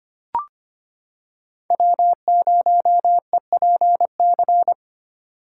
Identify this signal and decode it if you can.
Morse code, heard as a keyed tone, about 25 wpm, 700 Hz. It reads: W0EPC
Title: K1USN SST QSO Practice - 25wpm